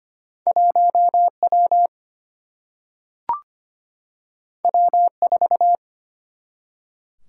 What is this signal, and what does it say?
Morse code, heard as a keyed tone, about 25 wpm, 700 Hz. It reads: 1W W4
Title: Single-Letter Single-Number 25wpm